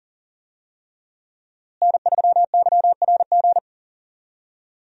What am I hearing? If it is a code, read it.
Morse code, heard as a keyed tone, about 40 wpm, 700 Hz. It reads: N3YRG
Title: USA Call Signs - 5 Characters - Repeated 3x - 40wpm